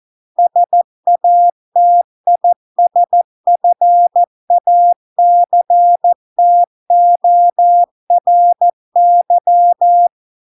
Morse code, heard as a keyed tone, about 14 wpm, 700 Hz. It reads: SATISFACTORY